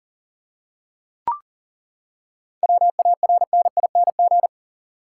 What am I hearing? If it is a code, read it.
Morse code, heard as a keyed tone, about 37 wpm, 700 Hz. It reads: WARNING
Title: Seven Letter Words - 40wpm